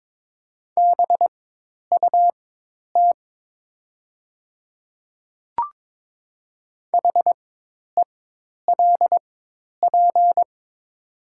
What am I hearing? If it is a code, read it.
Morse code, heard as a keyed tone, about 22 wpm, spaced as for 10 wpm, 700 Hz. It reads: BUT HELP
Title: 4x Character Spacing - Top 300 Words - 22wpm